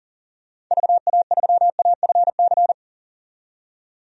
Morse code, heard as a keyed tone, about 40 wpm, 700 Hz. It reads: VA3AFC
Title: DXCC-100 Call Signs - 6 Characters - 40wpm